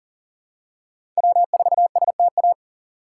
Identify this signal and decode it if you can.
Morse code, heard as a keyed tone, about 40 wpm, 700 Hz. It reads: W4STA
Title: USA Call Signs - 5 Characters - 3x Character Spacing - 40wpm